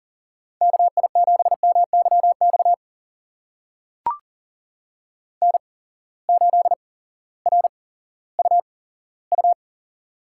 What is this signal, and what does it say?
Morse code, heard as a keyed tone, about 38 wpm, spaced as for 18 wpm, 700 Hz. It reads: KI7MYX N8RUU
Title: USA Call Signs - 8x Character Spacing - 40wpm